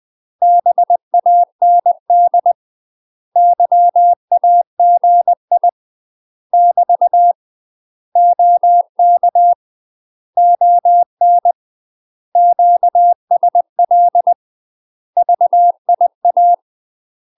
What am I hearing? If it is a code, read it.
Morse code, heard as a keyed tone, about 20 wpm, 700 Hz. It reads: BAND YAGI <BT> OK ON QSL VIA
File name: UK MASTER QSOs TRAINING / 20WPM / UK MASTER QSOs 0018